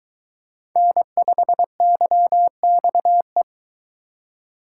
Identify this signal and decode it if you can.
Morse code, heard as a keyed tone, about 22 wpm, 700 Hz. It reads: N5YXE